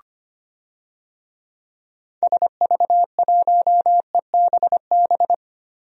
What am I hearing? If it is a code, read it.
Morse code, heard as a keyed tone, about 25 wpm, 700 Hz. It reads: SV1EBB
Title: DXCC-100 Call Signs - 6 Characters - 25wpm